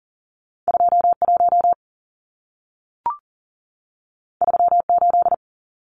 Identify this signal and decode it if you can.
Morse code, heard as a keyed tone, about 40 wpm, 700 Hz. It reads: 21 38
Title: Numbers - 2-Digit 40wpm